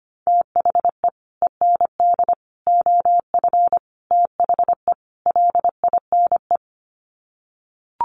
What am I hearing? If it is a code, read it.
Morse code, heard as a keyed tone, about 25 wpm, 700 Hz. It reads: THE END OF THE LINE
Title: Sets of 5 Words Encore2 25wpm